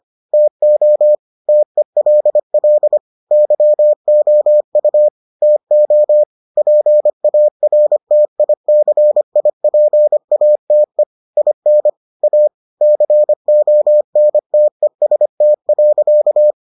Morse code, heard as a keyed tone, about 25 wpm, 600 Hz. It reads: TO TELL YOU TO PARTICIPATE IN A CONTEST.